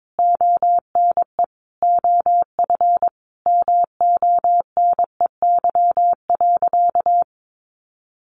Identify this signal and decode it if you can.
Morse code, heard as a keyed tone, about 22 wpm, 700 Hz. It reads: ONE OF MONEY.
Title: Sentences from Top 1400 Words 22wpm